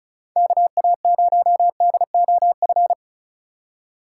Morse code, heard as a keyed tone, about 36 wpm, 700 Hz. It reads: KA0DOF
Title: USA Call Signs - 6 Characters - 2x Character Spacing - 35wpm